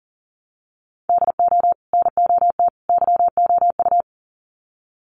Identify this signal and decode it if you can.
Morse code, heard as a keyed tone, about 40 wpm, 700 Hz. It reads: DO NOT YOU
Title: Sets of 3 Words 40wpm